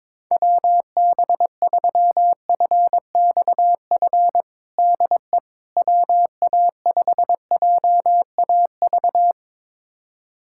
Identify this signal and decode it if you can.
Morse code, heard as a keyed tone, about 22 wpm, 700 Hz. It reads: WB3FXF DE WA5JAV
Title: USA Callsign DE Callsign 22wpm